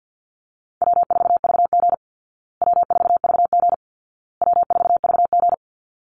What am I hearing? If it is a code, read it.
Morse code, heard as a keyed tone, about 50 wpm, 700 Hz. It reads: W4VG W4VG W4VG
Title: Easy Call Signs Repeated 3x - 50wpm